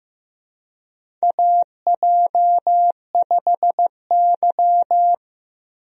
Morse code, heard as a keyed tone, about 15 wpm, 700 Hz. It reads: AJ5Y
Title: USA Call Signs - 4 Characters - Repeated 3x - 15wpm